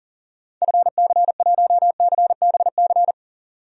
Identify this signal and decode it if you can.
Morse code, heard as a keyed tone, about 40 wpm, 700 Hz. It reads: FC1CBC
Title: DXCC-100 Call Signs - 6 Characters - 40wpm